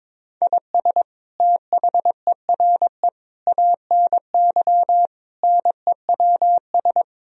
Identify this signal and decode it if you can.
Morse code, heard as a keyed tone, about 21 wpm, 700 Hz. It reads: IS THERE ANY NEWS
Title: Sentences from Top 1000 Words - Review Part 4 - 22wpm